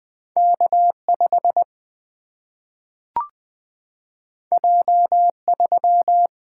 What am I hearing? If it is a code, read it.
Morse code, heard as a keyed tone, about 20 wpm, 700 Hz. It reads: K 5 J 3
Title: Single-Letter Single-Number 20wpm